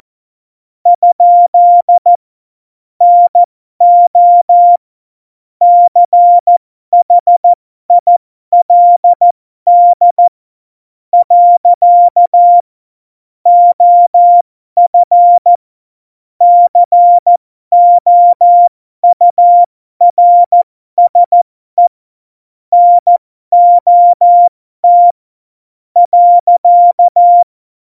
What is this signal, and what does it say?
Morse code, heard as a keyed tone, about 14 wpm, spaced as for 12 wpm, 700 Hz. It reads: ? NO CHILD . OF COURSE NOT .
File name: Morsekurs / Witze